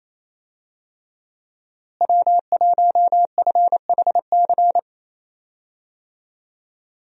Morse code, heard as a keyed tone, about 28 wpm, 700 Hz. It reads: W1FHC
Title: USA Call Signs - 5 Characters - Repeated 3x - 28wpm